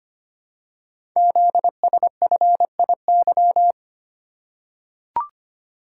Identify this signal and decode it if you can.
Morse code, heard as a keyed tone, about 25 wpm, 700 Hz. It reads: ZSFIY
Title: Code Groups - 5 Letters - 25wpm